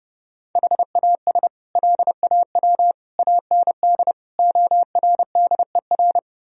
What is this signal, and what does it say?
Morse code, heard as a keyed tone, about 30 wpm, 700 Hz. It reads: HAS LAW AND ORDER
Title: Sentences from Top 1000 Words - Review Part 3 - 30wpm